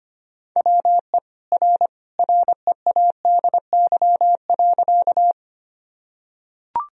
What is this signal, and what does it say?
Morse code, heard as a keyed tone, about 25 wpm, 700 Hz. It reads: WE R READY.
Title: Sentences from Top 400 Words and Abbreviations 25wpm